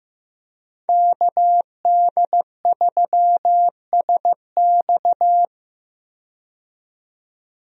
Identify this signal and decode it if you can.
Morse code, heard as a keyed tone, about 15 wpm, 700 Hz. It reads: KD3SX
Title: USA Call Signs - 5 Characters - 15wpm